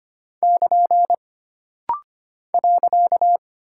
Morse code, heard as a keyed tone, about 25 wpm, 700 Hz. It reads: <KN> .
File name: Single Punctuation-Prosign - Rapid-Fire - 25wpm